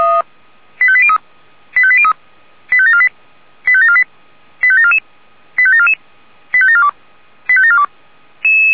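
Five-tone selective call (ZVEI), heard as quick runs of five tones, tones of about 70 ms. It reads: 85803 85803 85748 85748 85744 85744 85742 85742
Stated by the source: Five-tone ZVEI selective call received on an SDR.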